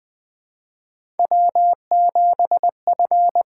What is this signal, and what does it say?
Morse code, heard as a keyed tone, about 20 wpm, 700 Hz. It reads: W7F
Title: USA Call Signs - 8x Character Spacing - 20wpm